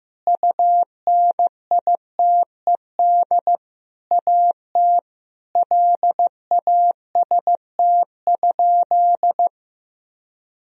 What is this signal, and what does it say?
Morse code, heard as a keyed tone, about 15 wpm, 700 Hz. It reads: UNITED AT LAST?
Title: Sentences from Top 1000 Words - Review Part 4 - 15wpm